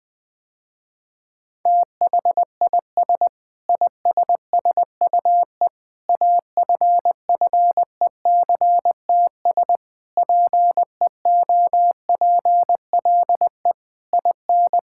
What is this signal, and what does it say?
Morse code, heard as a keyed tone, about 20 wpm, 700 Hz. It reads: THIS ISSUE AFFECTS PEOPLE IN
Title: Sentences with 8-Word Phrases - Spoken Beginning - 20wpm